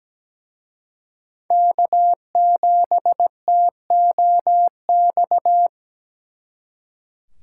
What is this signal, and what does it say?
Morse code, heard as a keyed tone, about 20 wpm, 700 Hz. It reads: K7TOX
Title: Speed Racing - USA 1x3 Call Signs 17wpm